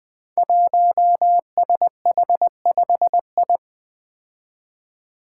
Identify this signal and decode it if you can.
Morse code, heard as a keyed tone, about 20 wpm, 700 Hz. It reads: 1SH5I
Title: Five Challenging Letter-Number Combinations 20wpm